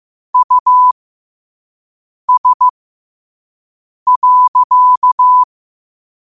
Morse code, heard as a keyed tone, about 15 wpm, spaced as for 5 wpm, 1000 Hz. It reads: US.
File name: qotdfiles / 5wpm